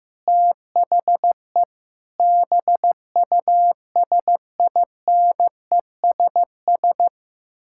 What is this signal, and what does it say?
Morse code, heard as a keyed tone, about 15 wpm, 700 Hz. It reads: THE BUSINESS
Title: Sentences from Top 1000 Words - Review Part 5 - 15wpm